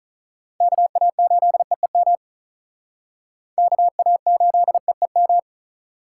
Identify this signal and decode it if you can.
Morse code, heard as a keyed tone, about 45 wpm, 700 Hz. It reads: KA8EEM KA8EEM